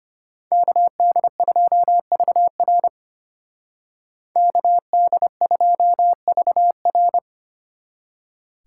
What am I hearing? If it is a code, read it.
Morse code, heard as a keyed tone, about 33 wpm, 700 Hz. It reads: KD2VR KD2VR